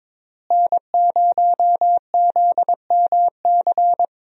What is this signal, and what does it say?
Morse code, heard as a keyed tone, about 22 wpm, 700 Hz. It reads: N0ZMC
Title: USA Call Signs - 5 Characters - Repeated 3x - 22wpm